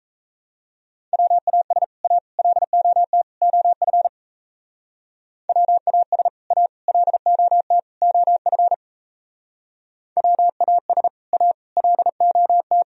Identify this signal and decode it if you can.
Morse code, heard as a keyed tone, about 37 wpm, 700 Hz. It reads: WAS A LOT OF WAS A LOT OF WAS A LOT